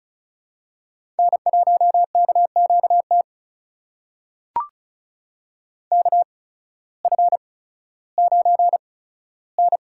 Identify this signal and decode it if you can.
Morse code, heard as a keyed tone, about 35 wpm, spaced as for 16 wpm, 700 Hz. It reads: N1KQT KF9N
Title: USA Call Signs - 5 Characters - 8x Character Spacing - 35wpm